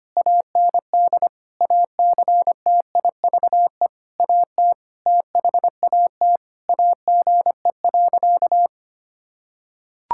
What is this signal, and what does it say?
Morse code, heard as a keyed tone, about 25 wpm, 700 Hz. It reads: AND ACTIVE AT THAT AGE.
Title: Sentences from Top 1000 Words - Review Part 6 - 25wpm